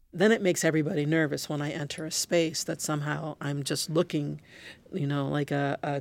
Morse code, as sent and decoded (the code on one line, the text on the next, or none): none